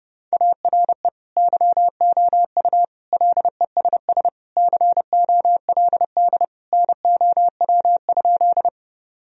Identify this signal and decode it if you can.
Morse code, heard as a keyed tone, about 30 wpm, 700 Hz. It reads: ARE YOU LESS COLD NOW?
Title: Sentences from Top 1000 Words - Review Part 2 - 30wpm